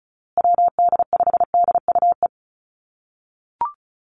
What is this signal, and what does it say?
Morse code, heard as a keyed tone, about 34 wpm, 700 Hz. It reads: WD5DUE
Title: USA Call Signs Repeated 3x 35wpm